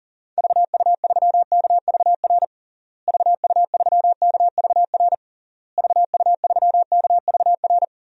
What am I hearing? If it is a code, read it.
Morse code, heard as a keyed tone, about 40 wpm, 700 Hz. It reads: VU3KVR VU3KVR VU3KVR
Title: DXCC-100 Call Signs - 6 Characters - Repeated 3x - 40wpm